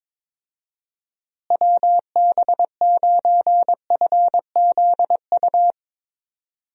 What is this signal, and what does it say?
Morse code, heard as a keyed tone, about 22 wpm, 700 Hz. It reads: WB9FZU
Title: USA Call Signs - 6 Characters - Repeated 2x - 22wpm